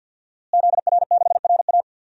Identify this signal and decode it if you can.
Morse code, heard as a keyed tone, about 47 wpm, 700 Hz. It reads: ZR6RA
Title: DXCC-100 Call Signs - 5 Characters - Repeated 3x - 50wpm